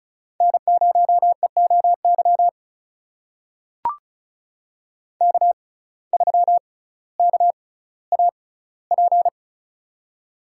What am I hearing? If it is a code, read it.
Morse code, heard as a keyed tone, about 36 wpm, spaced as for 18 wpm, 700 Hz. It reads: N0EOY K3KAP
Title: USA Call Signs - 5 Characters - 6x Character Spacing - 35wpm